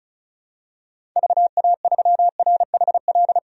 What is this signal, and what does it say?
Morse code, heard as a keyed tone, about 36 wpm, 700 Hz. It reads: VA3RHL
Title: DXCC-100 Call Signs - 6 Characters - 35wpm